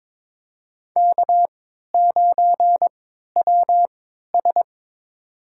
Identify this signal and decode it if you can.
Morse code, heard as a keyed tone, about 21 wpm, spaced as for 12 wpm, 700 Hz. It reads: K9WS
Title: USA Call Signs - 3x Character Spacing - 22wpm